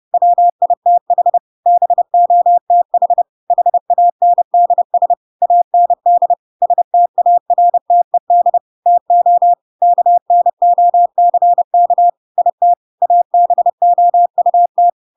Morse code, heard as a keyed tone, about 30 wpm, 700 Hz. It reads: WITH BOTH HANDS AND STARTED TO KNOCK IT ABOUT
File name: Pinocchio / Pinocchio ch1 30 30 ik1hge